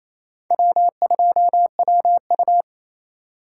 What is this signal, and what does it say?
Morse code, heard as a keyed tone, about 27 wpm, 700 Hz. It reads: W2WU
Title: USA Call Signs - 4 Characters - 3x Character Spacing - 28wpm